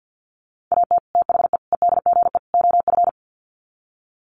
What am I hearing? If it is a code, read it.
Morse code, heard as a keyed tone, about 50 wpm, 700 Hz. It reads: AT THE EDGE OF
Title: Sets of 4 Words 50wpm